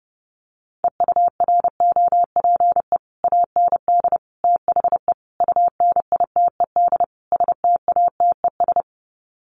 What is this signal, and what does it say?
Morse code, heard as a keyed tone, about 30 wpm, 700 Hz. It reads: EUROPE AND THE UNITED STATES
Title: Sets of 5 Words Encore2 30wpm